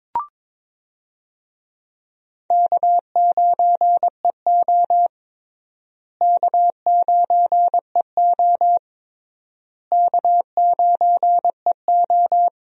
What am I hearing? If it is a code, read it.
Morse code, heard as a keyed tone, about 22 wpm, 700 Hz. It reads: K9EO K9EO K9EO
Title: USA Call Signs - 4 Characters - Repeated 3x - 22wpm